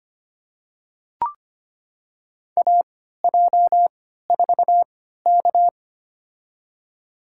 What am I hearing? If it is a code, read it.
Morse code, heard as a keyed tone, about 25 wpm, spaced as for 14 wpm, 700 Hz. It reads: AJ4K
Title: USA Call Signs - 4 Characters - 3x Character Spacing - 25wpm